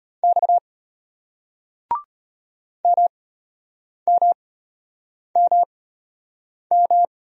Morse code, heard as a keyed tone, about 32 wpm, spaced as for 8 wpm, 700 Hz. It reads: X MMMM